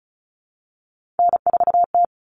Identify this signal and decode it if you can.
Morse code, heard as a keyed tone, about 34 wpm, 700 Hz. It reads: N4T
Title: All Cut Numbers - 3-Digit 35wpm